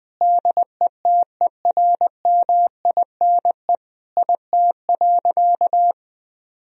Morse code, heard as a keyed tone, about 20 wpm, 700 Hz. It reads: DETERMINE IT.
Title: Sentences from Top 1000 Words - Review Part 4 - 20wpm